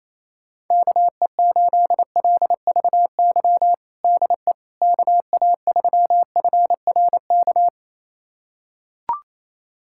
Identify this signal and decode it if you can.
Morse code, heard as a keyed tone, about 28 wpm, 700 Hz. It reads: KE8LVY DE KA3FRK
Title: USA Callsign DE Callsign - 6 Characters - 28wpm